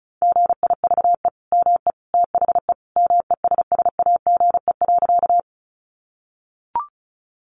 Morse code, heard as a keyed tone, about 34 wpm, 700 Hz. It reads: GIVE ME THE MESSAGE.
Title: Sentences from Top 1100 Words 35wpm